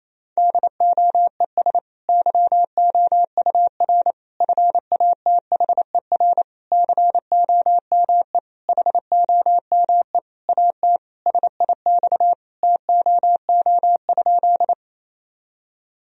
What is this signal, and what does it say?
Morse code, heard as a keyed tone, about 28 wpm, 700 Hz. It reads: DOES YOUR FATHER COME HOME AT SIX TOO?
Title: Sentences from Top 1000 Words - Review Part 2 - 28wpm